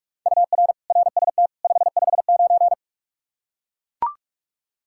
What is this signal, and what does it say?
Morse code, heard as a keyed tone, about 43 wpm, 700 Hz. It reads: UR RST 559
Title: K1USN SST QSO Practice 45wpm